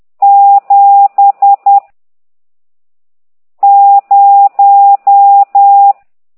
Morse code, heard as a keyed tone, about 10 wpm, 800 Hz. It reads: 7 0
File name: ct2idw / exerc15